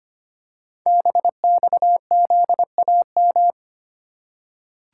Morse code, heard as a keyed tone, about 25 wpm, 700 Hz. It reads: BXZAM
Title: Code Groups - 5 Letters - 25wpm